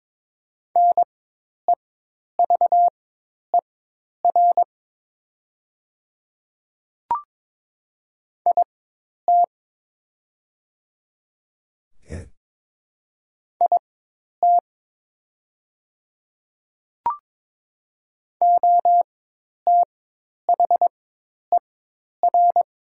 Morse code, heard as a keyed tone, about 22 wpm, spaced as for 10 wpm, 700 Hz. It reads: NEVER IT IT OTHER
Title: 4x Character Spacing - Top 300 Words - 22wpm